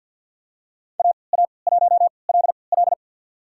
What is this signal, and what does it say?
Morse code, heard as a keyed tone, about 52 wpm, spaced as for 29 wpm, 700 Hz. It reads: AA1LL
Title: USA Call Signs - 5 Characters - 3x Character Spacing - 50wpm